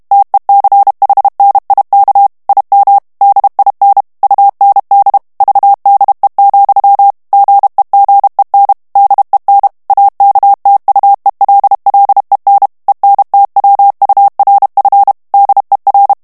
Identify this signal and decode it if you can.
Morse code, heard as a keyed tone, about 32 wpm, 800 Hz. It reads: TECHNIK IM DIN UND VDE, GEGEN DEN AKTUELLEN ENTWURF DER
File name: ALLEMAND texte clair 1 vitesse 30 mots minute